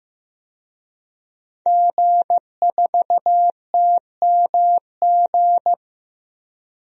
Morse code, heard as a keyed tone, about 15 wpm, 700 Hz. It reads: G4TMG
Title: DXCC-100 Call Signs - 5 Characters - Repeated 2x - 15wpm